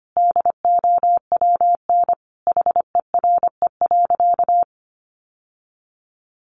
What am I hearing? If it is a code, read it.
Morse code, heard as a keyed tone, about 25 wpm, 700 Hz. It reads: DOWN HERE.
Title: Sentences from Top 1100 Words 25wpm